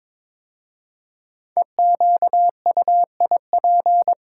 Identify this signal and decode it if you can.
Morse code, heard as a keyed tone, about 22 wpm, 700 Hz. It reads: EQUIP
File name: practice / 22wpm / 5 Letter Words - First 3 Letters Given - 22wpm